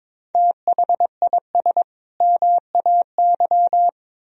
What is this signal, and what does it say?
Morse code, heard as a keyed tone, about 22 wpm, 700 Hz. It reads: THIS MAY
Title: Sentences from Top 1000 Words - Review Part 3 - 22wpm